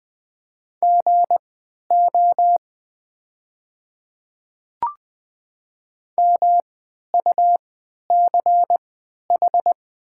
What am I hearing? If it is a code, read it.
Morse code, heard as a keyed tone, about 20 wpm, spaced as for 11 wpm, 700 Hz. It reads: GO MUCH
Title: 3x Character Spacing - Top 200 Words - 20wpm